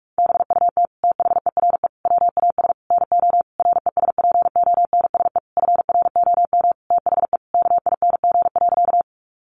Morse code, heard as a keyed tone, about 44 wpm, 700 Hz. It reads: BUT THERE WAS NO RESPONSE FROM THE KING.
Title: Sentences from Top 1400 Words 45wpm